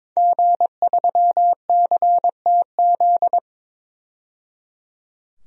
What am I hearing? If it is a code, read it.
Morse code, heard as a keyed tone, about 22 wpm, 700 Hz. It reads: G3CTZ
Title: DXCC-100 Call Signs - 5 Characters - 22wpm